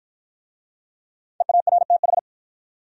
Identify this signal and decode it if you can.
Morse code, heard as a keyed tone, about 53 wpm, 700 Hz. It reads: EARTH